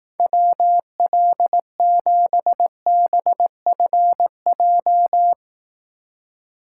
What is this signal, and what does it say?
Morse code, heard as a keyed tone, about 18 wpm, 700 Hz. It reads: WL7BFJ
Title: Speed Racing - USA 2x3 Call Signs 15wpm